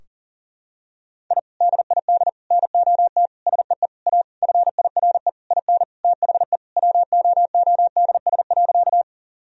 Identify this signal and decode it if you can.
Morse code, heard as a keyed tone, about 40 wpm, 700 Hz. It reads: I DID NOT SEE A FIRE IN THE WOODS.
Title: Sentences from Top 1000 Words - Review Part 2 - 40wpm